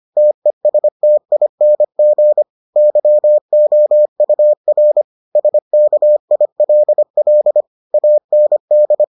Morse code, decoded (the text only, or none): TESTING YOUR SKILL AND